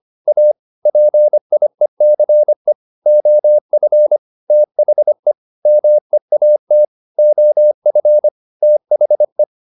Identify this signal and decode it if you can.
Morse code, heard as a keyed tone, about 25 wpm, 600 Hz. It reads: A PIECE OF THE MEAT OF THE